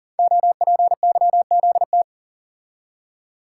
Morse code, heard as a keyed tone, about 41 wpm, 700 Hz. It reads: OPYZT
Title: Code Groups - 5 Letters - 40wpm